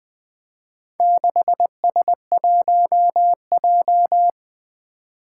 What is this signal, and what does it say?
Morse code, heard as a keyed tone, about 20 wpm, 700 Hz. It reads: 6S1J
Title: Four Challenging Letter-Number Combinations 20wpm